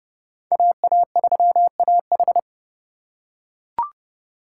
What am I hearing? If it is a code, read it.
Morse code, heard as a keyed tone, about 29 wpm, 700 Hz. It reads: AA3AH